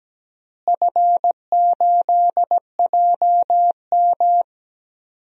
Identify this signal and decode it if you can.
Morse code, heard as a keyed tone, about 17 wpm, 700 Hz. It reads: F8JM
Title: DXCC-100 Call Signs - 4 Characters - Repeated 2x - 17wpm